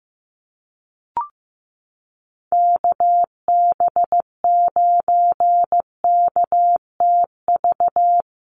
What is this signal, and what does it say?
Morse code, heard as a keyed tone, about 15 wpm, 700 Hz. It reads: KB9KTV
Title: USA Call Signs Repeated 2x 15wpm